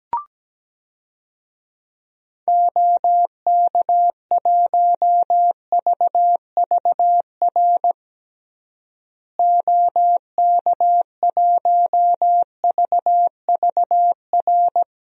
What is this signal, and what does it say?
Morse code, heard as a keyed tone, about 17 wpm, 700 Hz. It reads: OK1VVR OK1VVR
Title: DXCC-100 Call Signs - 6 Characters - Repeated 3x - 17wpm